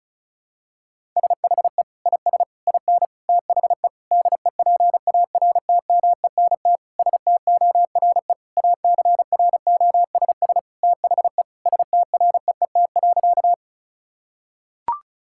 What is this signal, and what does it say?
Morse code, heard as a keyed tone, about 34 wpm, 700 Hz. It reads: SHE IS IN THE DEPARTMENT STORE ACROSS THE STREET.
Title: Sentences from Top 1000 Words - Review Part 5 - 35wpm